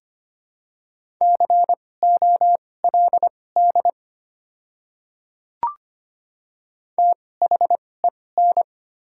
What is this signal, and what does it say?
Morse code, heard as a keyed tone, about 25 wpm, spaced as for 18 wpm, 700 Hz. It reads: COLD THEN
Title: 2x Character Spacing - Top 500 Words - 25wpm